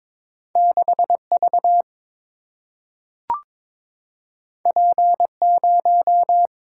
Morse code, heard as a keyed tone, about 21 wpm, 700 Hz. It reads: 6V P0
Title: Single-Letter Single-Number 22wpm